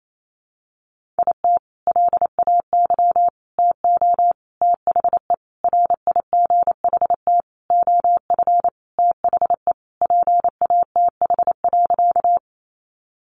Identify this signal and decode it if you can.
Morse code, heard as a keyed tone, about 28 wpm, 700 Hz. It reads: IT LAY TO THE RIGHT OF THE PATH.
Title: Sentences from Top 1400 Words 28wpm